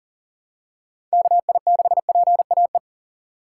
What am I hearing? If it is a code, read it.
Morse code, heard as a keyed tone, about 40 wpm, 700 Hz. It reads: KI6PAE
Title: USA Call Signs - 2x Character Spacing - 40wpm